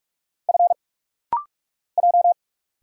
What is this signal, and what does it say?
Morse code, heard as a keyed tone, about 44 wpm, 700 Hz. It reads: F J
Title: Single Letter-Number Alt - Rapid-Fire 45wpm